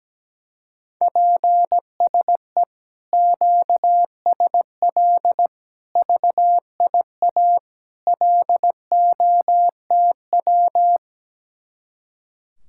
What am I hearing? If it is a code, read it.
Morse code, heard as a keyed tone, about 17 wpm, 700 Hz. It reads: PSE QSL VIA LOTW
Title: Speed-Racing Common QSO Phrases Encore - 17wpm